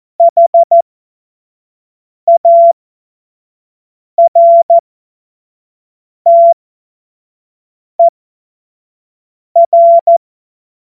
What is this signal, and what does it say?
Morse code, heard as a keyed tone, about 14 wpm, spaced as for 5 wpm, 680 Hz. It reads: HARTER
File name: Morsekurs / Witze / Witz 20. Juni 2023 25 F70